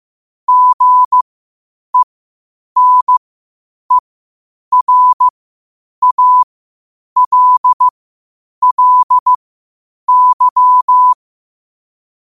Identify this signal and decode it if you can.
Morse code, heard as a keyed tone, about 15 wpm, spaced as for 8 wpm, 1000 Hz. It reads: GENERALLY
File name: qotdfiles / 8wpm